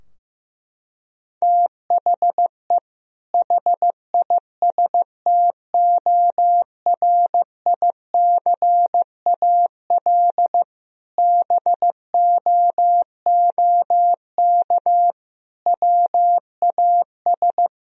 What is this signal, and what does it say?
Morse code, heard as a keyed tone, about 15 wpm, 700 Hz. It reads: THE HISTORICAL BOOK WAS
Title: Sentences from Top 1000 Words - Review Part 4 - 15wpm